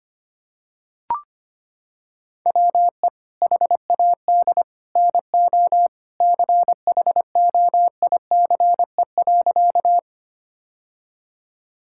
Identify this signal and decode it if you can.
Morse code, heard as a keyed tone, about 25 wpm, 700 Hz. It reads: WE HAD NO CHOICE.
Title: Sentences from Top 1000 Words - Review Part 6 - 25wpm